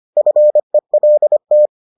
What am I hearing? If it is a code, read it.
Morse code, heard as a keyed tone, about 25 wpm, 600 Hz. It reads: FELT